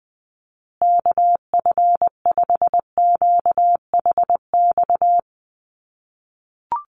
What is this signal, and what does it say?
Morse code, heard as a keyed tone, about 20 wpm, 700 Hz. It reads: KF5QHX
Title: USA Call Signs Repeated 2x 20wpm